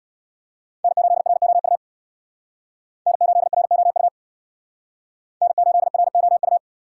Morse code, heard as a keyed tone, about 62 wpm, 700 Hz. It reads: N8ROU N8ROU N8ROU
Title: Speed Racing - USA 1x3 Call Signs 50wpm